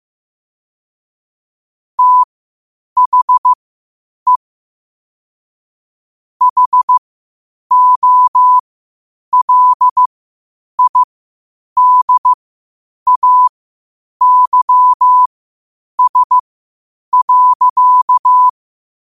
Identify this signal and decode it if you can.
Morse code, heard as a keyed tone, about 15 wpm, spaced as for 8 wpm, 1000 Hz. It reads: THE HOLIDAYS.
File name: qotdfiles / 8wpm